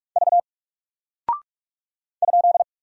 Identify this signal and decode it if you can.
Morse code, heard as a keyed tone, about 44 wpm, 700 Hz. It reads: V ?
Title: Single Character (Letter:Number:Punctuation:Prosign) Alt 45wpm